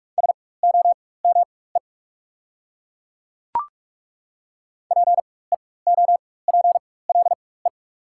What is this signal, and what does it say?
Morse code, heard as a keyed tone, about 42 wpm, spaced as for 20 wpm, 690 Hz. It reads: SOME PEOPLE
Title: 4x Character Spacing - Top 200 Words - 45wpm